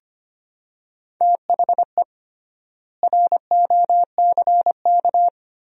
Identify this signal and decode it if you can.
Morse code, heard as a keyed tone, about 25 wpm, 700 Hz. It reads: THE ROCK
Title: 3x Word-Spacing - Sentences from Top 400 Words - 25wpm